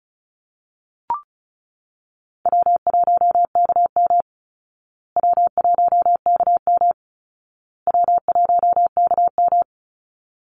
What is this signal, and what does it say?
Morse code, heard as a keyed tone, about 34 wpm, 700 Hz. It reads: W1KM W1KM W1KM
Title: Call Signs Repeated 2x - 35wpm